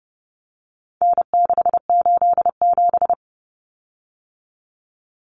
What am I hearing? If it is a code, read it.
Morse code, heard as a keyed tone, about 30 wpm, 700 Hz. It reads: N687
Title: Common Cut Numbers - 4-Digit 30wpm